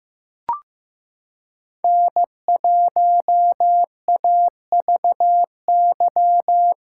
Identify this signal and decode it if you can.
Morse code, heard as a keyed tone, about 15 wpm, 700 Hz. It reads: N1AVY
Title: USA Callsign DE Callsign - 5 Characters - 15wpm